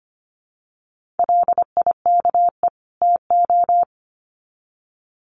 Morse code, heard as a keyed tone, about 25 wpm, 700 Hz. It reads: LIKE TO
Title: Sets of 2 Words 25wpm